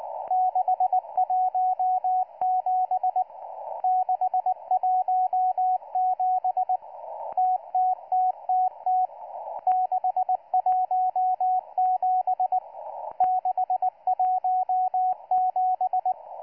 Morse code, heard as a keyed tone, about 19 wpm, 740 Hz, with radio noise. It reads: 617 617 TTTTT 617 617